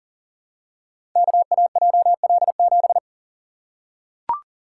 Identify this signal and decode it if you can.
Morse code, heard as a keyed tone, about 42 wpm, 700 Hz. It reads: KAJL7
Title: Code Groups - 5 Letter:Numbers - 40wpm